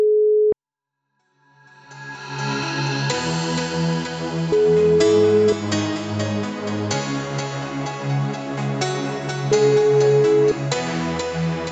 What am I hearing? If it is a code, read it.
Ringback tone (Europe, 425 Hz), with noise.